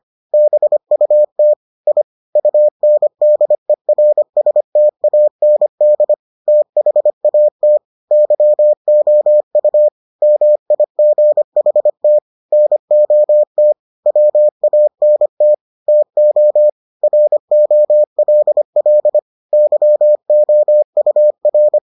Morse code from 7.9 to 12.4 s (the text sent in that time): YOU MIGHT